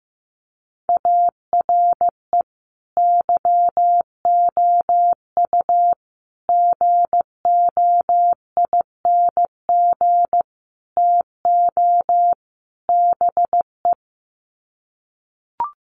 Morse code, heard as a keyed tone, about 15 wpm, 700 Hz. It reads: ARE YOU GOING TO BE
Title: Sets of 5 Words Encore 15wpm